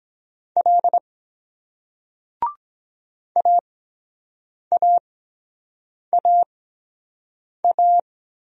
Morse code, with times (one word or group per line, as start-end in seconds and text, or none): 0.6-1.0 L
3.4-8.0 AAAA